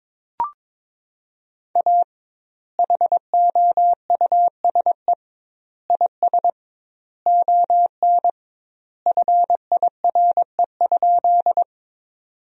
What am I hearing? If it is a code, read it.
Morse code, heard as a keyed tone, about 22 wpm, 700 Hz. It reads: A HOUSE IS ON FIRE?
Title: 2x Word-Spacing - Sentences from Top 400 Words - 22wpm